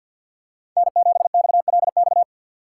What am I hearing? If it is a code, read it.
Morse code, heard as a keyed tone, about 47 wpm, 700 Hz. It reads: N7XLX
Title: USA Call Signs - 5 Characters - 6x Character Spacing - 50wpm